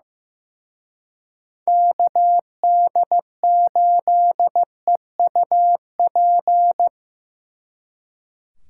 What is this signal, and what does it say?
Morse code, heard as a keyed tone, about 18 wpm, 700 Hz. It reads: KD8EUP
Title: Speed Racing - USA 2x3 Call Signs 15wpm